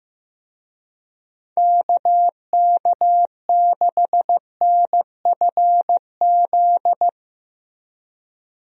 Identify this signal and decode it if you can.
Morse code, heard as a keyed tone, about 15 wpm, 700 Hz. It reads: KK6NFZ
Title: USA Call Signs - 6 Characters - Repeated 2x - 15wpm